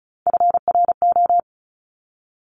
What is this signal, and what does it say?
Morse code, heard as a keyed tone, about 36 wpm, 700 Hz. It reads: FRO
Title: Top Three Letter Combinations in Words 35wpm